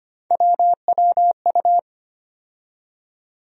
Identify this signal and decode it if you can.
Morse code, heard as a keyed tone, about 25 wpm, 700 Hz. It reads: WWU
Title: USA Call Sign Postfixes - 3 Letters 25wpm